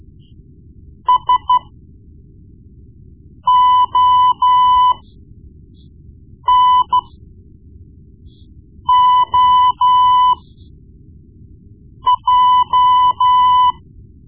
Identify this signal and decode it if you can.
Morse code, heard as a keyed tone, about 14 wpm, spaced as for 4 wpm, 1000 Hz. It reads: SONOJ
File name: Morse Lessons / Morse Lessons / Tape-1B